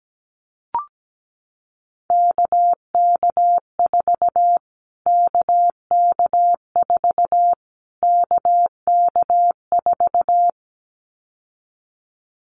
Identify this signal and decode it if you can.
Morse code, heard as a keyed tone, about 17 wpm, 700 Hz. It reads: KK4 KK4 KK4
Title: Callsign Prefixes for Top 25 DXCC - Repeated 3x - 17wpm